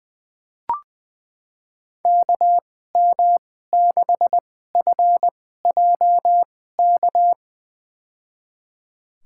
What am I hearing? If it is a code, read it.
Morse code, heard as a keyed tone, about 20 wpm, spaced as for 14 wpm, 700 Hz. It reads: KM6FJK
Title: USA Call Signs - 6 Characters - 2x Character Spacing - 20wpm